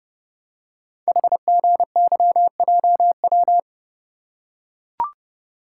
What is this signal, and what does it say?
Morse code, heard as a keyed tone, about 30 wpm, 700 Hz. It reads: HGYJW
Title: Code Groups - 5 Letters - 30wpm